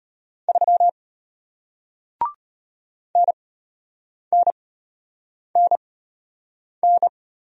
Morse code, heard as a keyed tone, about 33 wpm, spaced as for 8 wpm, 700 Hz. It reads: 3 NNNN